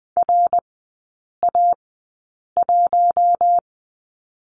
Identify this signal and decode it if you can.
Morse code, heard as a keyed tone, about 20 wpm, spaced as for 8 wpm, 700 Hz. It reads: RA1
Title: 166 Lesson 27e - Number 7 - Three Characters 20wpm